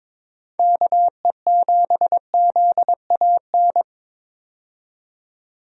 Morse code, heard as a keyed tone, about 22 wpm, 700 Hz. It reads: KE7ZAN
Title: USA Call Signs - 6 Characters - 22wpm